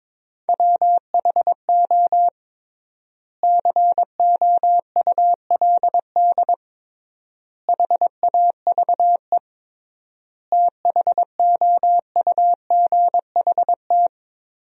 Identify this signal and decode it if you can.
Morse code, heard as a keyed tone, about 22 wpm, 700 Hz. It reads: WHO COULD HAVE THOUGHT
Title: 3x Word-Spacing - Sentences from Top 300 Words - 22wpm